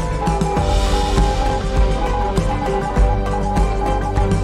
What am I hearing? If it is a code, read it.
Morse code, heard as a keyed tone, about 20 wpm, 820 Hz. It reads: GO AHEAD